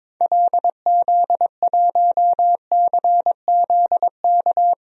Morse code, heard as a keyed tone, about 21 wpm, 700 Hz. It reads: LZ1CZK
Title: DXCC-100 Call Signs - 6 Characters - Repeated 3x - 22wpm